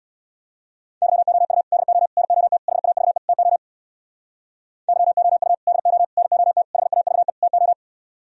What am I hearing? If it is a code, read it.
Morse code, heard as a keyed tone, about 64 wpm, 700 Hz. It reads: YOU DO NOT HAVE TO YOU DO NOT HAVE TO